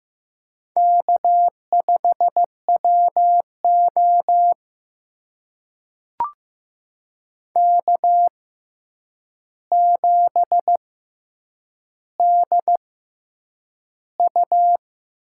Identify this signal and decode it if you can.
Morse code, heard as a keyed tone, about 15 wpm, spaced as for 8 wpm, 700 Hz. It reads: K5WO K7DU
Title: USA Call Signs - 4 Characters - 6x Character Spacing - 15wpm